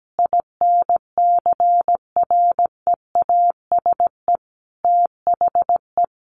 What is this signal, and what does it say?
Morse code, heard as a keyed tone, about 17 wpm, 700 Hz. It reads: INCREASE THE
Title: Sentences from Top 1300 Words 17wpm